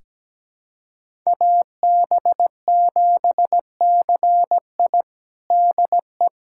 Morse code, heard as a keyed tone, about 17 wpm, 700 Hz. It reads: AB7CI DE
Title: USA Callsign DE Callsign 17wpm